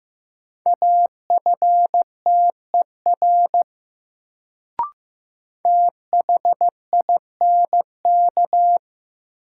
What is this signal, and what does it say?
Morse code, heard as a keyed tone, about 15 wpm, 700 Hz. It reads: AFTER THINK
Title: Sets of 2 Words - New Word Drill - 15wpm